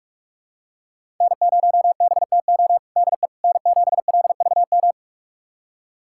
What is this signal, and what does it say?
Morse code, heard as a keyed tone, about 41 wpm, 700 Hz. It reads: N0BTO DE N7LVM
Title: USA Callsign DE Callsign - 5 Characters - 45wpm